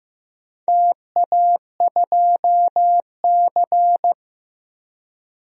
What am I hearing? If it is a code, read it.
Morse code, heard as a keyed tone, about 15 wpm, 700 Hz. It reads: TA2C
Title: DXCC-100 Call Signs - 4 Characters - 15wpm